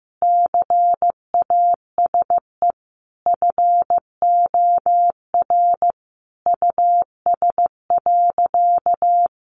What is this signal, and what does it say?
Morse code, heard as a keyed tone, about 15 wpm, 700 Hz. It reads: CASE FOR US.
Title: Sentences from Top 1100 Words 15wpm